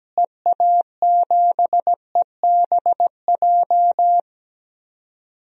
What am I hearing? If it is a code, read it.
Morse code, heard as a keyed tone, about 17 wpm, 700 Hz. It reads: EA7EBJ
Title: DXCC-100 Call Signs - 6 Characters - Repeated 2x - 17wpm